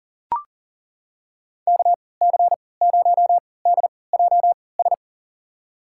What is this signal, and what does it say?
Morse code, heard as a keyed tone, about 40 wpm, spaced as for 23 wpm, 700 Hz. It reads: KC0DJS
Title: USA Call Signs - 6 Characters - 3x Character Spacing - 40wpm